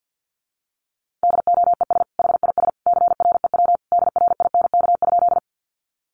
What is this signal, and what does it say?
Morse code, heard as a keyed tone, about 48 wpm, 700 Hz. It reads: DOES HIS CREW DRINK?
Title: Sentences from Top 1500 Words 50wpm